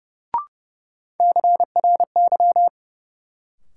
Morse code, heard as a keyed tone, about 30 wpm, 700 Hz. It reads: CRY
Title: Sentences from Top 500 Words - New Word Drill - 30wpm